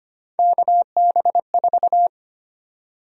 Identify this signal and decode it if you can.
Morse code, heard as a keyed tone, about 25 wpm, 700 Hz. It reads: KB4
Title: USA Call Sign Prefixes - Repeated 2x - 25wpm